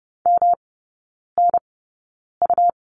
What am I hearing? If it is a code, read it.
Morse code, heard as a keyed tone, about 30 wpm, spaced as for 9 wpm, 700 Hz. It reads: MNU
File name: Lesson 18e - Letter m - Three Characters 30wpm